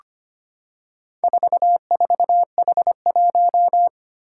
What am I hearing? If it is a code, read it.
Morse code, heard as a keyed tone, about 25 wpm, 700 Hz. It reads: 44H1
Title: Four Challenging Letter-Number Combinations 25wpm